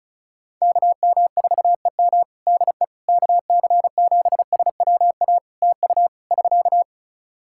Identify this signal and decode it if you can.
Morse code, heard as a keyed tone, about 35 wpm, 700 Hz. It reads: KM4EM DE KC7SWA TU <SK>
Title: Practice QSO - Exchange 4 - Ending - 35wpm